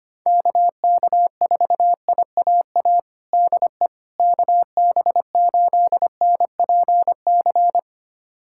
Morse code, heard as a keyed tone, about 25 wpm, 700 Hz. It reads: KK4IAA DE KB8NPC
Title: USA Callsign DE Callsign - 6 Characters - 25wpm